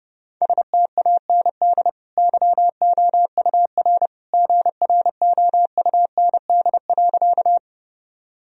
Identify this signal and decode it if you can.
Morse code, heard as a keyed tone, about 30 wpm, 700 Hz. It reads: STAND YOUR GROUND.
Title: Sentences from Top 1000 Words - Review Part 2 - 30wpm